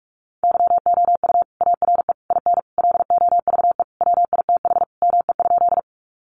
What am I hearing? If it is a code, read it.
Morse code, heard as a keyed tone, about 44 wpm, 700 Hz. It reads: YOU ARE IN LOVE WITH ME?
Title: Sentences from Top 400 Words 45wpm